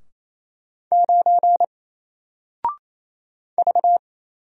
Morse code, heard as a keyed tone, about 28 wpm, 700 Hz. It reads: 9 V
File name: Single-Letter-Number-28wpm